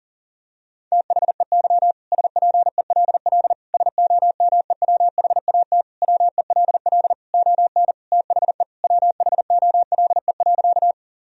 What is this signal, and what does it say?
Morse code, heard as a keyed tone, about 40 wpm, 700 Hz. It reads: THEY SPELL SOMEWHAT WELL ON THE WHOLE.
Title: Sentences from Top 1000 Words - Review Part 1 - 40wpm